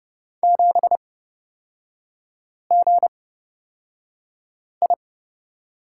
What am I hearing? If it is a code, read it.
Morse code, heard as a keyed tone, about 30 wpm, spaced as for 5 wpm, 700 Hz. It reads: 7GI
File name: Lesson 36e - BT - Three Characters 30wpm